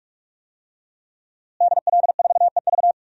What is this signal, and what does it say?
Morse code, heard as a keyed tone, about 42 wpm, 700 Hz. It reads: DL4EV
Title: DXCC-100 Call Signs - 5 Characters - Repeated 3x - 45wpm